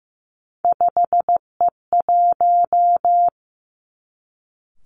Morse code, heard as a keyed tone, about 15 wpm, 700 Hz. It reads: 5E1
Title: Three Challenging Letter-Number Combinations 15wpm